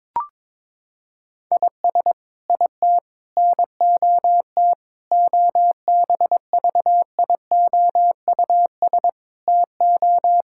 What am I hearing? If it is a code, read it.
Morse code, heard as a keyed tone, about 22 wpm, 700 Hz. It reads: IS IT NOT OBVIOUS TO
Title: Sentences from Top 1600 Words 22wpm